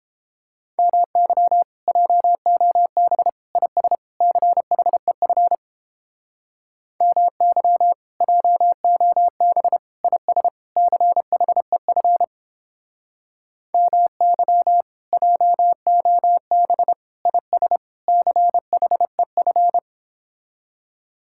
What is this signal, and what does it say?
Morse code, heard as a keyed tone, about 29 wpm, 700 Hz. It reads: MY JOB IS CHEF MY JOB IS CHEF MY JOB IS CHEF